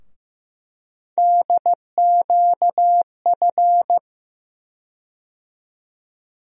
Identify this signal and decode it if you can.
Morse code, heard as a keyed tone, about 15 wpm, 700 Hz. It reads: DQF
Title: USA Call Sign Postfixes - 3 Letters 15wpm